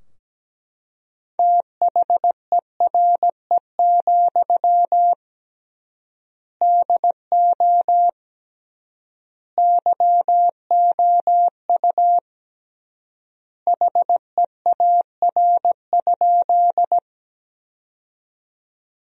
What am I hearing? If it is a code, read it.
Morse code, heard as a keyed tone, about 17 wpm, 700 Hz. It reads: THERE, DO YOU HEAR?
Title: 3x Word-Spacing - Sentences from Top 300 Words - 17wpm